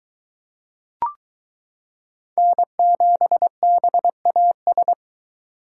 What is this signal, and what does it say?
Morse code, heard as a keyed tone, about 23 wpm, 700 Hz. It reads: N7BAS